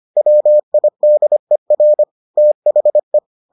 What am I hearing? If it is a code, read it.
Morse code, heard as a keyed tone, about 25 wpm, 600 Hz. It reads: WIDER THE